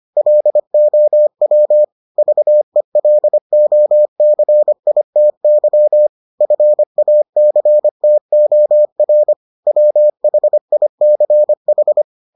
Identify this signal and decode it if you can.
Morse code, heard as a keyed tone, about 25 wpm, 600 Hz. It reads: LOW VELOCITY FACTOR WHICH